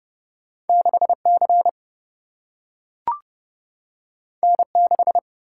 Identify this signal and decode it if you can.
Morse code, heard as a keyed tone, about 29 wpm, 700 Hz. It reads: 6C N6
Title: Single-Letter Single-Number 30wpm